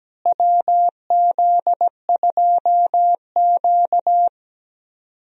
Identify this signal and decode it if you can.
Morse code, heard as a keyed tone, about 17 wpm, 700 Hz. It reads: WZ2Q
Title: USA Call Signs - 4 Characters - Repeated 2x - 17wpm